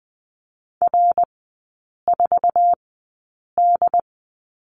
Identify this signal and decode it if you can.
Morse code, heard as a keyed tone, about 20 wpm, spaced as for 8 wpm, 700 Hz. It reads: R4D
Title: 178 Lesson 29e - Slash - Three Characters 20wpm